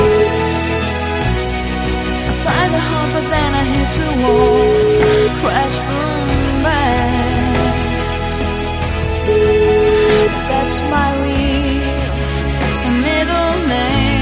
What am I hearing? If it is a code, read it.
Ringback tone (Europe, 425 Hz), with noise.